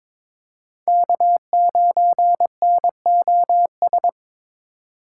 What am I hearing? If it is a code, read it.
Morse code, heard as a keyed tone, about 26 wpm, 700 Hz. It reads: K9NOS
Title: Speed Racing - USA 1x3 Call Signs 22wpm